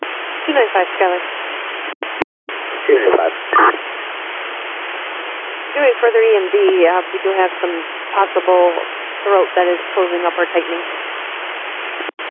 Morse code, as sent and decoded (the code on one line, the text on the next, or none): none